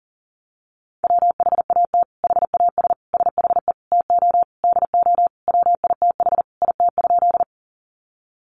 Morse code, heard as a keyed tone, about 40 wpm, 700 Hz. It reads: WHAT HAS SHE TO DO WITH IT?
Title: Sentences from Top 100 Words 40wpm